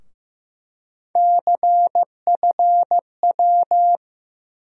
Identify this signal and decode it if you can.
Morse code, heard as a keyed tone, about 15 wpm, 700 Hz. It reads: CFW
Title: USA Call Sign Postfixes - 3 Letters 15wpm